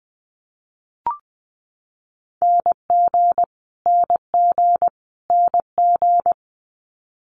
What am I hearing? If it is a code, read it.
Morse code, heard as a keyed tone, about 20 wpm, 700 Hz. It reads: NG NG NG
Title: Multi-letter Phonemes Repeated 3x 20wpm